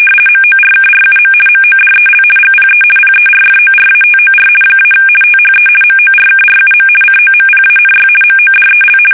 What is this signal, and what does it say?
RTTY, about 50 baud. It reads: MORKRUM MODEL GPE PERFORATOR - THE MORKRUM COMPANY MODEL
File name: fsk850-50B / teletype-corp